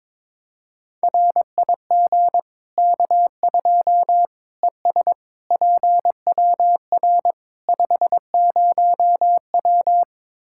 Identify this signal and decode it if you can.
Morse code, heard as a keyed tone, about 22 wpm, 700 Hz. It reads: RIG K2 ES PWR 50W
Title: Practice QSO - Exchange 2 - Info - 22wpm